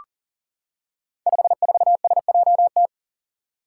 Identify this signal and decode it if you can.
Morse code, heard as a keyed tone, about 42 wpm, 700 Hz. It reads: 54SJT
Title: Five Challenging Letter-Number Combinations 40wpm